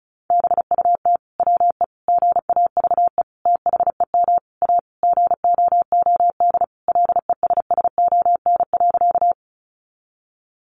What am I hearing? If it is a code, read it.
Morse code, heard as a keyed tone, about 34 wpm, 700 Hz. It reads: BUT WE GAVE THEM A GOOD LESSON.
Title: Sentences from Top 1200 Words 35wpm